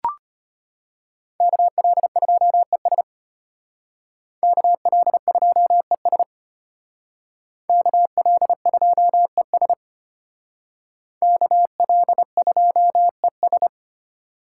Morse code, heard as a keyed tone, about 32 wpm, 700 Hz. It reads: KL2ES KL2ES KL2ES KL2ES